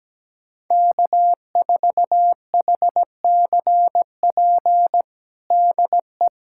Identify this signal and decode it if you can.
Morse code, heard as a keyed tone, about 17 wpm, 700 Hz. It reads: K4HCP DE
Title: USA Callsign DE Callsign - 5 Characters - 17wpm